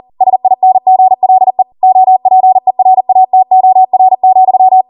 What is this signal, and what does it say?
Morse code, heard as a keyed tone, about 40 wpm, 750 Hz. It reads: SINGLE OPERATOR,